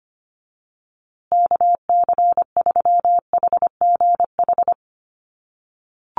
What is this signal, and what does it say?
Morse code, heard as a keyed tone, about 25 wpm, 700 Hz. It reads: KC3HGH
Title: USA Call Signs Repeated 2x 25wpm